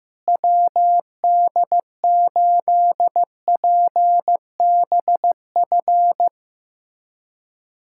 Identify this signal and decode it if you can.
Morse code, heard as a keyed tone, about 15 wpm, 700 Hz. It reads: WD8PBF
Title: USA Callsign DE Callsign - 6 Characters - 15wpm